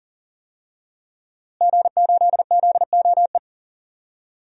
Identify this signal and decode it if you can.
Morse code, heard as a keyed tone, about 37 wpm, 700 Hz. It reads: G8ZOE
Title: DXCC-100 Call Signs - 5 Characters - Repeated 2x - 40wpm